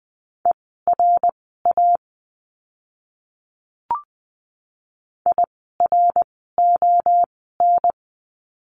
Morse code, heard as a keyed tone, about 20 wpm, spaced as for 14 wpm, 700 Hz. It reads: ERA IRON
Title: 054 Lesson 10f - Letter r - Words 20wpm